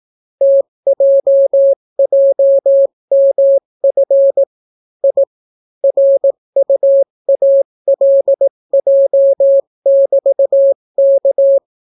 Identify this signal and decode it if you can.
Morse code, heard as a keyed tone, about 18 wpm, 550 Hz. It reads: TJJMF I RUALJ<BT>K